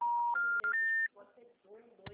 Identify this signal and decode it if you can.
Special information tone, with speech.